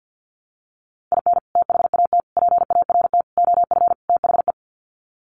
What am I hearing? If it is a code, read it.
Morse code, heard as a keyed tone, about 48 wpm, 700 Hz. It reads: IN THAT PART OF THE
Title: Sets of 5 Words Encore2 50wpm